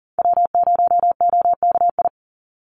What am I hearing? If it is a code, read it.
Morse code, heard as a keyed tone, about 40 wpm, 700 Hz. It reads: W0OKI
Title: USA Call Signs 40wpm